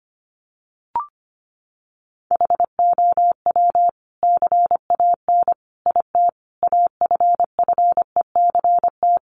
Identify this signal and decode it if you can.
Morse code, heard as a keyed tone, about 25 wpm, 700 Hz. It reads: HOW CAN IT AFFECT
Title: Sentences from Top 900 Words 25wpm